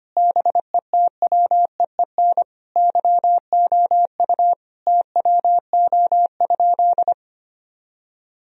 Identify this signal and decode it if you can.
Morse code, heard as a keyed tone, about 25 wpm, 700 Hz. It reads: BETWEEN YOU TWO?
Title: Sentences from Top 1000 Words - Review Part 1 - 25wpm